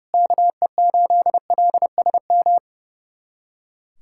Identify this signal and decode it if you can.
Morse code, heard as a keyed tone, about 32 wpm, 700 Hz. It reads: KE8LSM
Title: USA Callsign DE Callsign - 6 Characters - 30wpm